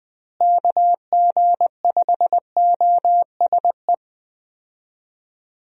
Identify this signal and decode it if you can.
Morse code, heard as a keyed tone, about 20 wpm, 700 Hz. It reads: KG5OSE
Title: USA Call Signs - 6 Characters - Repeated 2x - 20wpm